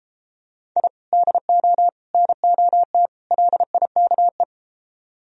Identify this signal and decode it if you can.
Morse code, heard as a keyed tone, about 32 wpm, 700 Hz. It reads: I DO NOT LIKE